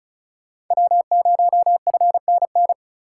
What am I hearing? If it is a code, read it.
Morse code, heard as a keyed tone, about 36 wpm, 700 Hz. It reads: W0FNN
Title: USA Call Signs - 5 Characters - 3x Character Spacing - 35wpm